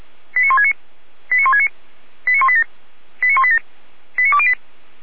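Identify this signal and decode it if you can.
Five-tone selective call (ZVEI), heard as quick runs of five tones, tones of about 70 ms. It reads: 89169 89169 89187 89187 89208
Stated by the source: ZVEI five-tone selective call received on an SDR.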